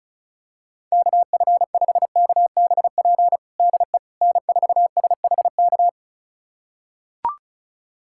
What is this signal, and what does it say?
Morse code, heard as a keyed tone, about 34 wpm, 700 Hz. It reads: KF5KBP DE N4SHK
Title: USA Callsign DE Callsign 35wpm